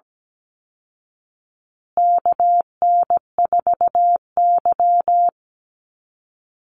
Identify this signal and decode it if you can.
Morse code, heard as a keyed tone, about 17 wpm, 700 Hz. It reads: KN4Y
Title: Call Signs Repeated 3x - 17wpm